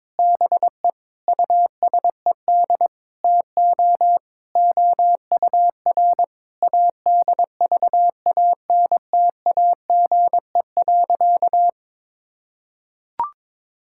Morse code, heard as a keyed tone, about 22 wpm, 700 Hz. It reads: BE USED TO OUR ADVANTAGE.
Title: Sentences from Top 1000 Words - Review Part 6 - 22wpm